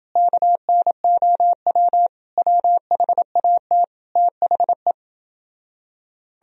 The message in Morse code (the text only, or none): KNOW WHAT THE